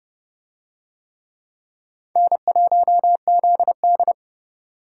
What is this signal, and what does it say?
Morse code, heard as a keyed tone, about 27 wpm, 700 Hz. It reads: N1ZD